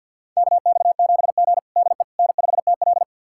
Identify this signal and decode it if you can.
Morse code, heard as a keyed tone, about 50 wpm, 700 Hz. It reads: KX7G DE N5TL
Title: USA Callsign DE Callsign - 4 Characters - 50wpm